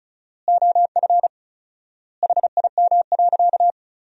Morse code, heard as a keyed tone, about 34 wpm, 700 Hz. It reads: OF HIM.
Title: 4x Word-Spacing - Sentences from Top 400 Words - 35wpm